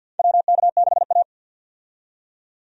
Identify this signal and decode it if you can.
Morse code, heard as a keyed tone, about 53 wpm, 700 Hz. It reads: WK6A
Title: USA Call Signs - 4 Characters - 50wpm